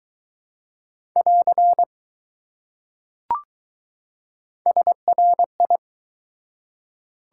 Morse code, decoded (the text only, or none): <AR> SRI